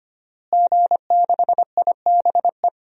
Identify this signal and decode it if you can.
Morse code, heard as a keyed tone, about 25 wpm, 700 Hz. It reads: G6IBE
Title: DXCC-100 Call Signs - 5 Characters - Repeated 2x - 25wpm